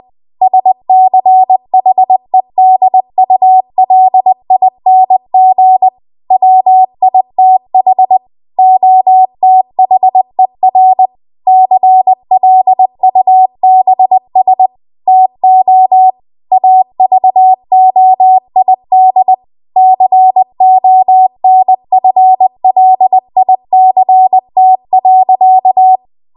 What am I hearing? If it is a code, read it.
Morse code, heard as a keyed tone, about 20 wpm, 750 Hz. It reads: SCHEDULING WITH OTHER CLUBS TO AVOID CONFLICT.